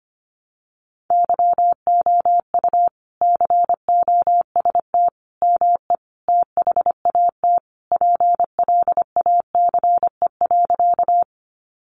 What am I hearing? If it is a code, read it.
Morse code, heard as a keyed tone, about 25 wpm, 700 Hz. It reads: YOU COST ME THAT PLACE.
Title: Sentences from Top 800 Words 25wpm